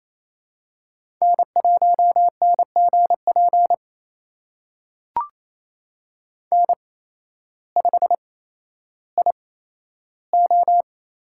Morse code, heard as a keyed tone, about 28 wpm, spaced as for 13 wpm, 700 Hz. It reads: N1NGP N5IO
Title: USA Call Signs - 5 Characters - 8x Character Spacing - 28wpm